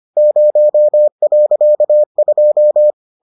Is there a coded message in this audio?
Yes: Morse code from 0.2 to 2.9 s, heard as a keyed tone, about 25 wpm, 600 Hz. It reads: 0.2